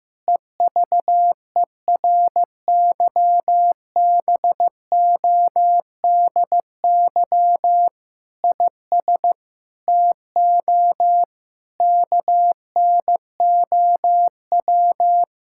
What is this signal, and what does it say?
Morse code, heard as a keyed tone, about 15 wpm, 700 Hz. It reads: EVERYBODY IS TO KNOW
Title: Sentences from Top 1600 Words 15wpm